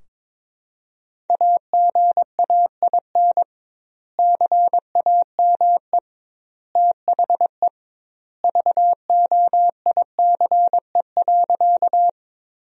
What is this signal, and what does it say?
Morse code, heard as a keyed tone, about 21 wpm, 700 Hz. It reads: AGAIN CAME THE VOICE.
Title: 2x Word-Spacing - Sentences from Top 500 Words - 22wpm